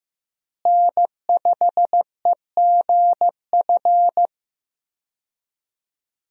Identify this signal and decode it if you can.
Morse code, heard as a keyed tone, about 15 wpm, 700 Hz. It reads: N5EGF
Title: USA Call Signs - 5 Characters - Repeated 2x - 15wpm